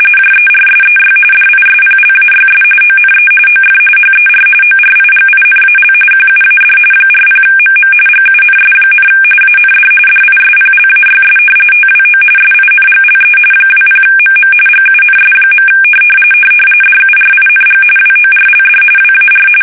RTTY, about 45.45 baud. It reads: JERSEY DISPLAYS ONE OF HER ORIGINAL QUAD 40MM GUNS, AND DISPLAYS TWO SINGLE 20MM GUNS. DURING WORLD WAR
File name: fsk850 / battleshipnj